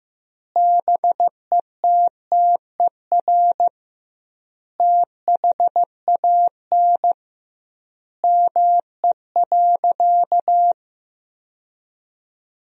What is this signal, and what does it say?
Morse code, heard as a keyed tone, about 15 wpm, 700 Hz. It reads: BETTER THAN ME.
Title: 2x Word-Spacing - Sentences from Top 400 Words - 15wpm